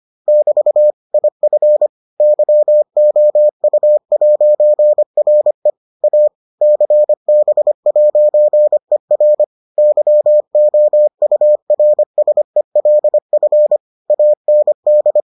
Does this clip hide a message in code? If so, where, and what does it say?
Yes: Morse code from 0.3 to 15.2 s, heard as a keyed tone, about 25 wpm, 600 Hz. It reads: <BT> IF YOU'RE A CB'ER YOURSELF AND